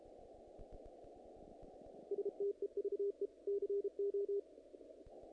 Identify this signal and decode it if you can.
Morse code, heard as a keyed tone, about 32 wpm, 400 Hz. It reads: STEVE CO
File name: contests / naqp-cw-k5tr-op-k5pi- -1900